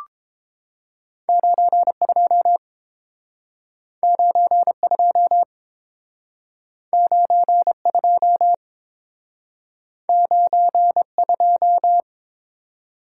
Morse code, heard as a keyed tone, about 30 wpm, 700 Hz. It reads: 92 92 92 92